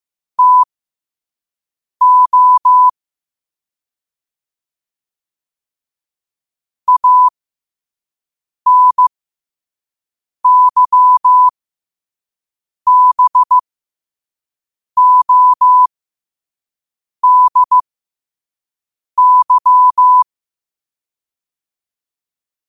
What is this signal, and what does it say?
Morse code, heard as a keyed tone, about 15 wpm, spaced as for 5 wpm, 1000 Hz. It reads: TO ANYBODY
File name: qotdfiles / 5wpm